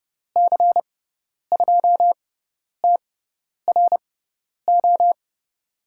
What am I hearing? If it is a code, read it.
Morse code, heard as a keyed tone, about 30 wpm, spaced as for 10 wpm, 700 Hz. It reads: C2TRO
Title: USA Call Signs - 6 Characters - 6x Character Spacing - 30wpm